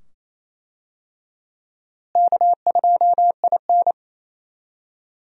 Morse code, heard as a keyed tone, about 27 wpm, 700 Hz. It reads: K2IN
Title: USA Call Signs - 4 Characters - Repeated 2x - 28wpm